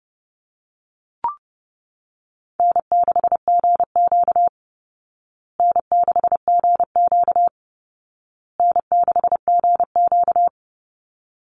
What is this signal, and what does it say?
Morse code, heard as a keyed tone, about 30 wpm, 700 Hz. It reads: N6GQ N6GQ N6GQ
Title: Easy Call Signs Repeated 3x - 30wpm